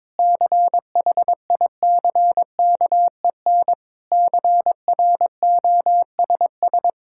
Morse code, heard as a keyed tone, about 22 wpm, 700 Hz. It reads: CHICKEN CROSS
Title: Sentences from Top 1000 Words - Review Part 5 - 22wpm